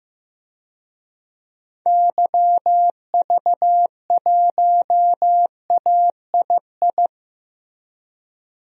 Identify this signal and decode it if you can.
Morse code, heard as a keyed tone, about 15 wpm, 700 Hz. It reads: YV1AII
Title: DXCC-100 Call Signs - 6 Characters - Repeated 2x - 15wpm